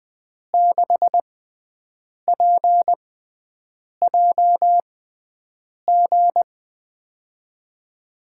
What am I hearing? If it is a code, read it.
Morse code, heard as a keyed tone, about 20 wpm, spaced as for 7 wpm, 700 Hz. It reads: 6PJG
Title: USA Call Signs - 5 Characters - 6x Character Spacing - 20wpm